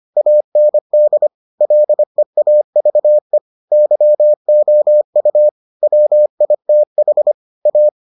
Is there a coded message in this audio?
Yes: Morse code from 0.2 to 7.9 s, heard as a keyed tone, about 25 wpm, 600 Hz. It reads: AND LEAVE YOU WITH A